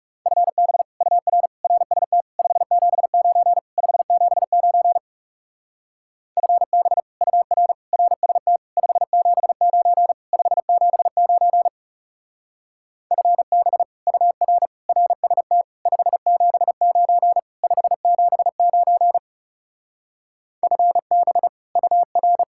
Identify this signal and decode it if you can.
Morse code, heard as a keyed tone, about 38 wpm, 700 Hz. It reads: FB UR RST 579 579 FB UR RST 579 579 FB UR RST 579 579 FB UR
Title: Speed-Racing Common QSO Phrases - 30wpm